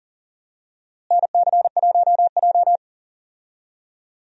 Morse code, heard as a keyed tone, about 40 wpm, 700 Hz. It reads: NC1J
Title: USA Call Signs - 4 Characters - Repeated 3x - 40wpm